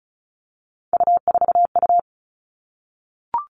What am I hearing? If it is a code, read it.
Morse code, heard as a keyed tone, about 34 wpm, 700 Hz. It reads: U4U
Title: All Cut Numbers - 3-Digit 35wpm